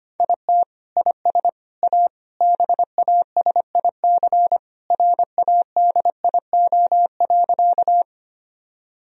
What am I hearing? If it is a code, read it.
Morse code, heard as a keyed tone, about 25 wpm, 700 Hz. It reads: IT IS A BASIC RADIO.
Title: Sentences from Top 1000 Words - Review Part 3 - 25wpm